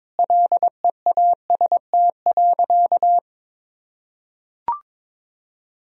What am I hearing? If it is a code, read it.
Morse code, heard as a keyed tone, about 22 wpm, 700 Hz. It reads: LEAST.
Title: Sentences from Top 1000 Words - Review Part 4 - 22wpm